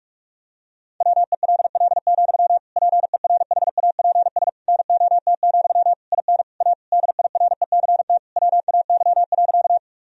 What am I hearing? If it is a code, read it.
Morse code, heard as a keyed tone, about 46 wpm, 700 Hz. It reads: WELL, PERHAPS NOT, IN A DIRECT WAY.
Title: Sentences from Top 1000 Words - Review Part 2 - 45wpm